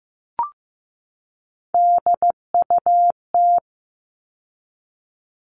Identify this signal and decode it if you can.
Morse code, heard as a keyed tone, about 15 wpm, 700 Hz. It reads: DUT
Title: All Cut Numbers - 3-Digit 15wpm